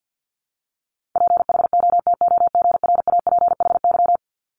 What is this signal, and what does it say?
Morse code, heard as a keyed tone, about 50 wpm, 690 Hz. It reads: PHOTOGRAPHY
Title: Multi-letter Phonemes 50wpm